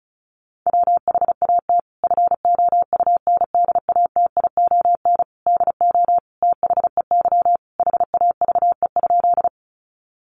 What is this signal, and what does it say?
Morse code, heard as a keyed tone, about 34 wpm, 700 Hz. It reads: WHAT FOUNDATION DO THEY HAVE?
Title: Sentences from Top 1200 Words 35wpm